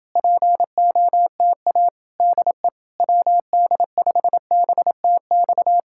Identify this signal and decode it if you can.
Morse code, heard as a keyed tone, about 30 wpm, 700 Hz. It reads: POTA DE WD5BTX